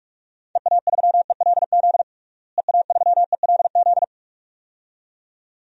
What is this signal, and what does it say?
Morse code, heard as a keyed tone, about 46 wpm, 700 Hz. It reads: EA3ELZ EA3ELZ
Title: DXCC-100 Call Signs - 6 Characters - Repeated 2x - 45wpm